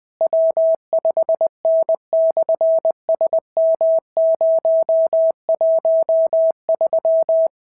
Morse code, heard as a keyed tone, about 20 wpm, 650 Hz. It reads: W5N/SM013
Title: SOTA QSO Practice 20wpm